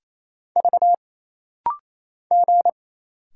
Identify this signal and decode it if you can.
Morse code, heard as a keyed tone, about 28 wpm, 700 Hz. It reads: V G
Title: Single Letter-Number Alt - Rapid-Fire 28wpm